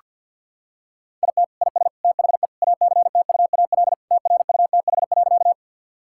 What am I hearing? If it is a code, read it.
Morse code, heard as a keyed tone, about 48 wpm, 700 Hz. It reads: IT IS THE ACTUAL TRUTH.
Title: Sentences from Top 1000 Words - Review Part 5 - 50wpm